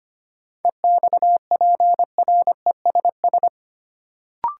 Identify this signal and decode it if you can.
Morse code, heard as a keyed tone, about 25 wpm, 700 Hz. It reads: EXPRESS
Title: Sentences from Top 1100 Words - New Word Drill Alt - 25wpm